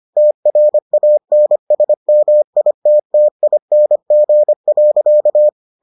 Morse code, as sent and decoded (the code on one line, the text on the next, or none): - .-. .- -. ... -- .. - - .. -. --. .-.-.-
TRANSMITTING.